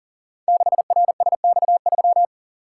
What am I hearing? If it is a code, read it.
Morse code, heard as a keyed tone, about 43 wpm, 700 Hz. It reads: 6RSX3
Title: Code Groups - 5 Letter:Numbers - 40wpm